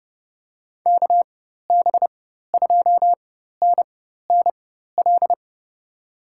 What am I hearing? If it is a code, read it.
Morse code, heard as a keyed tone, about 30 wpm, spaced as for 14 wpm, 700 Hz. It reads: KB2NNL
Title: USA Call Signs - 6 Characters - 4x Character Spacing - 30wpm